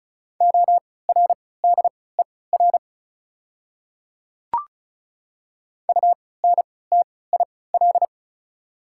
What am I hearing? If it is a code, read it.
Morse code, heard as a keyed tone, about 36 wpm, spaced as for 20 wpm, 690 Hz. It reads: ORDER UNTIL
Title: 3x Character Spacing - Top 500 Words - 35wpm